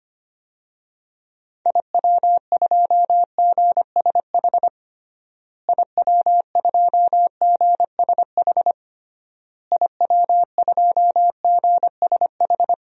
Morse code, heard as a keyed tone, about 25 wpm, 700 Hz. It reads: IW2GSH IW2GSH IW2GSH
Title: DXCC-100 Call Signs - 6 Characters - Repeated 3x - 25wpm